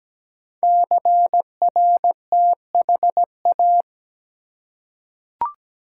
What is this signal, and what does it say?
Morse code, heard as a keyed tone, about 17 wpm, 700 Hz. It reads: CRTHA
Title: Code Groups - 5 Letters - 17wpm